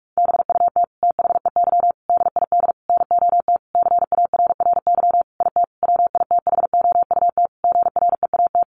Morse code, heard as a keyed tone, about 44 wpm, 700 Hz. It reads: BUT THEY DID NOT CARRY IT WITHOUT GREAT
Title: Sentences from Top 1400 Words 45wpm